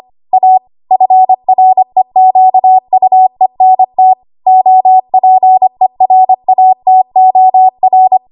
Morse code, heard as a keyed tone, about 25 wpm, 750 Hz. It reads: A FREQUENT OPERATOR
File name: Archive / 25 WPM / 25WPM